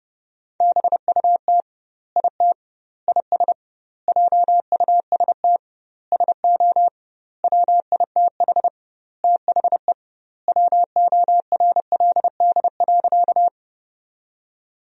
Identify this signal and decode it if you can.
Morse code, heard as a keyed tone, about 30 wpm, 700 Hz. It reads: BUT IT IS JUST SO WITH THE WORLD.
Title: 2x Word-Spacing - Sentences from Top 200 Words - 30wpm